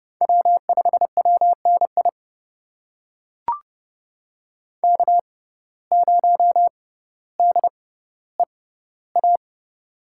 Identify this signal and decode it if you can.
Morse code, heard as a keyed tone, about 31 wpm, spaced as for 15 wpm, 700 Hz. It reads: W5WNI K0DEA
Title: USA Call Signs - 5 Characters - 6x Character Spacing - 30wpm